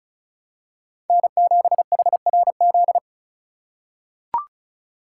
Morse code, heard as a keyed tone, about 34 wpm, 700 Hz. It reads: N7HRZ
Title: USA Call Signs - 5 Characters - 3x Character Spacing - 35wpm